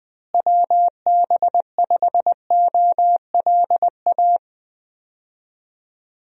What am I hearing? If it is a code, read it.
Morse code, heard as a keyed tone, about 20 wpm, 700 Hz. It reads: WB5OLA
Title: USA Call Signs - 6 Characters - Repeated 2x - 20wpm